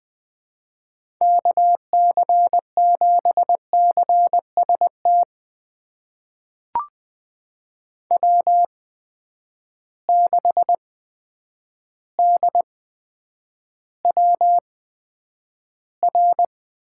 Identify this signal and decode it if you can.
Morse code, heard as a keyed tone, about 20 wpm, spaced as for 9 wpm, 700 Hz. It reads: KC7CST W6DWR
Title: USA Call Signs - 8x Character Spacing - 20wpm